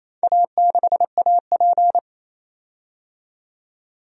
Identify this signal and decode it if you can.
Morse code, heard as a keyed tone, about 27 wpm, 700 Hz. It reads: A6AP
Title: DXCC-100 Call Signs - 4 Characters - 28wpm